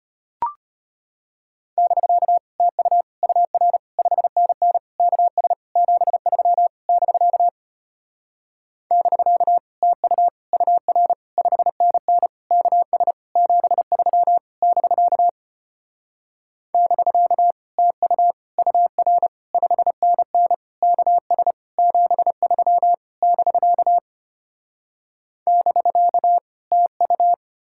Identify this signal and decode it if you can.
Morse code, heard as a keyed tone, about 33 wpm, 700 Hz. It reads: <BK> TU UR 5NN KS 73 <BK> <BK> TU UR 5NN KS 73 <BK> <BK> TU UR 5NN KS 73 <BK> <BK> TU